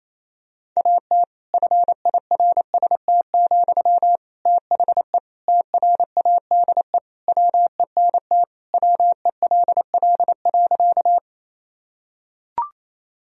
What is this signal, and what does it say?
Morse code, heard as a keyed tone, about 27 wpm, 700 Hz. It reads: AT FIRST, THE TRADE WENT WELL.
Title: Sentences from Top 1000 Words - Review Part 4 - 28wpm